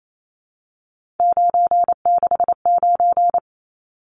Morse code, heard as a keyed tone, about 28 wpm, 700 Hz. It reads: 969
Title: Numbers - 3-Digit 28wpm